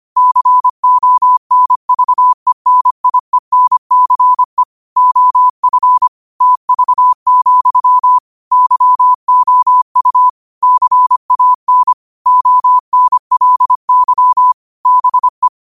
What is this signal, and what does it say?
Morse code, heard as a keyed tone, about 25 wpm, 1000 Hz. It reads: CONVENIENCE OF TV, YOU CAN ONLY BE
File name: qotdfiles / 25wpm